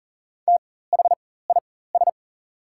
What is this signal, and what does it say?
Morse code, heard as a keyed tone, about 40 wpm, spaced as for 19 wpm, 700 Hz. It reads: THIS
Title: 4x Character Spacing - Top 300 Words - 40wpm